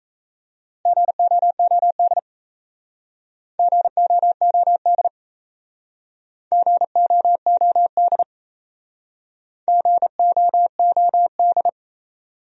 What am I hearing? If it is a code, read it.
Morse code, heard as a keyed tone, about 37 wpm, 690 Hz. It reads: GOOD GOOD GOOD GOOD